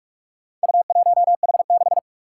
Morse code, heard as a keyed tone, about 45 wpm, 700 Hz. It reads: U1H6
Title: Four Challenging Letter-Number Combinations 45wpm